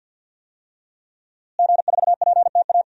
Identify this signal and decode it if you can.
Morse code, heard as a keyed tone, about 50 wpm, 700 Hz. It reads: G4PTA
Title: DXCC-100 Call Signs - 5 Characters - Repeated 3x - 50wpm